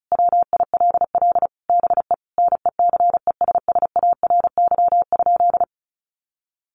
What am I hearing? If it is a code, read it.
Morse code, heard as a keyed tone, about 34 wpm, 700 Hz. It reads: WILL BE NECESSARY?
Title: Sentences from Top 800 Words 35wpm